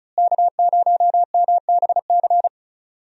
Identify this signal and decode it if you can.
Morse code, heard as a keyed tone, about 34 wpm, 700 Hz. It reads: K0MBC
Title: USA Call Signs - 5 Characters - Repeated 3x - 35wpm